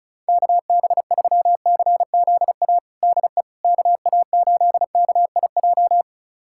Morse code, heard as a keyed tone, about 36 wpm, 700 Hz. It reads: KB3CZA DE KA8KIJ
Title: USA Callsign DE Callsign - 6 Characters - 35wpm